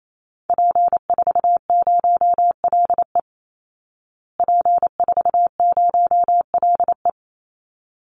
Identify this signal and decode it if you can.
Morse code, heard as a keyed tone, about 28 wpm, 690 Hz. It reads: P40LE P40LE
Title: Call Signs Repeated 3x - 28wpm